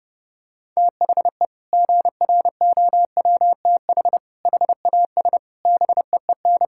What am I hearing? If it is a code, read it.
Morse code, heard as a keyed tone, about 30 wpm, 700 Hz. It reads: THE GROWTH HAS BEEN
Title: Sentences from Top 1000 Words - Review Part 5 - 30wpm